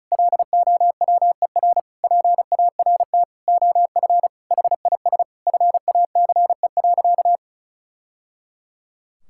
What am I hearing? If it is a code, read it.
Morse code, heard as a keyed tone, about 36 wpm, 700 Hz. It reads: LOWER PART OF HIS FACE.
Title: Sentences from Top 1000 Words - Review Part 5 - 35wpm